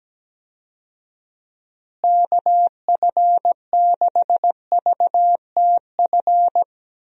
Morse code, heard as a keyed tone, about 17 wpm, 700 Hz. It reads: KF6VTF
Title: USA Call Signs - 6 Characters - Repeated 3x - 17wpm